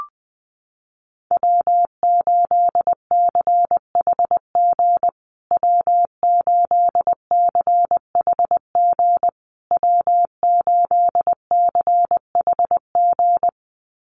Morse code, heard as a keyed tone, about 20 wpm, 700 Hz. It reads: W8CHG W8CHG W8CHG
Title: USA Call Signs Repeated 3x 20wpm